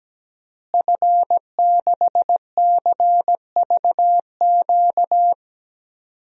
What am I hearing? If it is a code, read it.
Morse code, heard as a keyed tone, about 17 wpm, 700 Hz. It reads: F6CVQ
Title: DXCC-100 Call Signs - 5 Characters - Repeated 3x - 17wpm